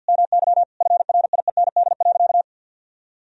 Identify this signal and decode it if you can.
Morse code, heard as a keyed tone, about 50 wpm, 700 Hz. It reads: MY FRIEND.
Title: Sentences from Top 1000 Words - Review Part 4 - 50wpm